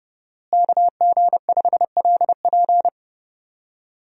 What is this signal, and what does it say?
Morse code, heard as a keyed tone, about 30 wpm, 700 Hz. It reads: KG5LP
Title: USA Call Signs - 5 Characters - Repeated 3x - 30wpm